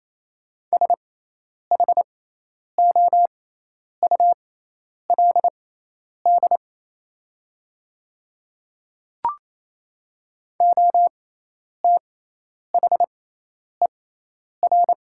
Morse code, heard as a keyed tone, about 28 wpm, spaced as for 10 wpm, 700 Hz. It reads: SHOULD OTHER
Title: 6x Character Spacing - Top 300 Words - 28wpm